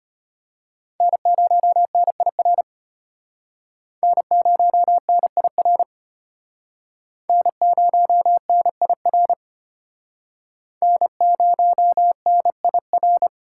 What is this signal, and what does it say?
Morse code, heard as a keyed tone, about 31 wpm, 700 Hz. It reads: N0NIR N0NIR N0NIR N0NIR
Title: Speed Racing - USA Call Signs 25wpm